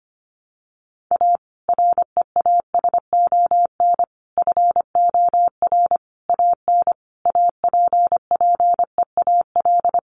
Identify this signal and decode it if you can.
Morse code, heard as a keyed tone, about 25 wpm, 700 Hz. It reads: A REASON FOR AN APPEAL
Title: Sentences from Top 1300 Words 25wpm